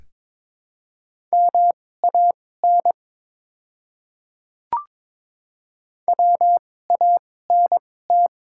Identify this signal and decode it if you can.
Morse code, heard as a keyed tone, about 22 wpm, spaced as for 16 wpm, 690 Hz. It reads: MAN WANT
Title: 2x Character Spacing - Top 200 Words - 22wpm